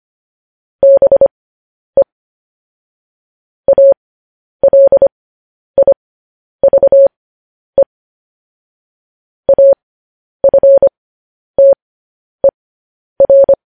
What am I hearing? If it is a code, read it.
Morse code, heard as a keyed tone, about 25 wpm, spaced as for 10 wpm, 550 Hz. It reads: BE ALIVE AFTER
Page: News Headlines in Morse Code at 10 WPM